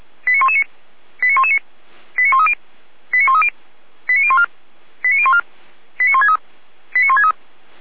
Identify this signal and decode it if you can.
Five-tone selective call (ZVEI), heard as quick runs of five tones, tones of about 80 ms. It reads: 89119 89119 89130 89130 89014 89014 89163 89163